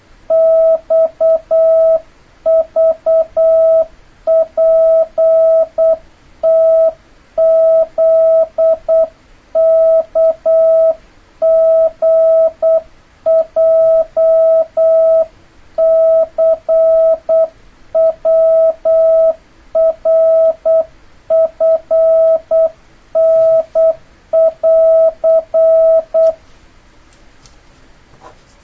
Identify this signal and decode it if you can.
Morse code, heard as a keyed tone, about 8 wpm, 650 Hz. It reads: XVPTZKGJCWRFN <AR>
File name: jh8cbh / 2026cw